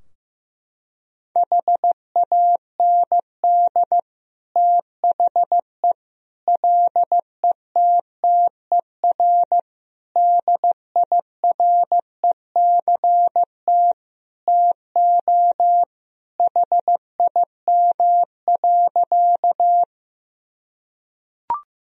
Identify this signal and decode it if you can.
Morse code, heard as a keyed tone, about 15 wpm, 700 Hz. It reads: HAND THE LETTER DIRECT TO HIM.
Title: Sentences from Top 1000 Words - Review Part 2 - 15wpm